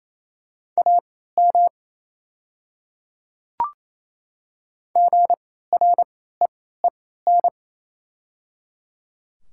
Morse code, heard as a keyed tone, about 28 wpm, spaced as for 16 wpm, 700 Hz. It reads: AM GREEN
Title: 3x Character Spacing - Top 500 Words - 28wpm